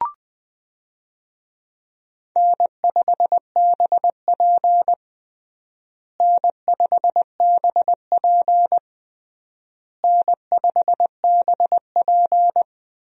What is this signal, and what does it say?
Morse code, heard as a keyed tone, about 20 wpm, 700 Hz. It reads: N5BP N5BP N5BP
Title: USA Call Signs - 4 Characters - Repeated 3x - 20wpm